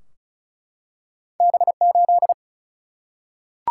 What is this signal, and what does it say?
Morse code, heard as a keyed tone, about 36 wpm, 700 Hz. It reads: B 8
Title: Single-Letter Single-Number 35wpm